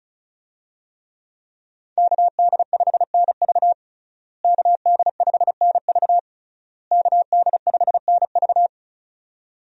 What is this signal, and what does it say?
Morse code, heard as a keyed tone, about 34 wpm, 700 Hz. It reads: KD5NV KD5NV KD5NV
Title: USA Call Signs - 5 Characters - Repeated 3x - 35wpm